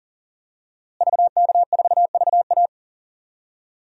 Morse code, heard as a keyed tone, about 37 wpm, 700 Hz. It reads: VK4VA
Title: DXCC-100 Call Signs - 5 Characters - Repeated 3x - 40wpm